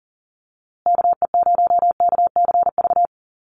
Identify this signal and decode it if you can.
Morse code, heard as a keyed tone, about 40 wpm, 700 Hz. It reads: KE0KCV
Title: USA Call Signs Repeated 3x 40wpm